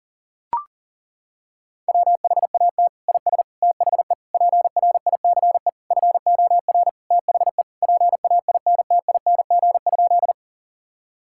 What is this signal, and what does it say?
Morse code, heard as a keyed tone, about 40 wpm, 700 Hz. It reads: WHAT IS THE PRICE FOR THE PAINTING?
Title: Sentences from Top 1000 Words - Review Part 6 - 40wpm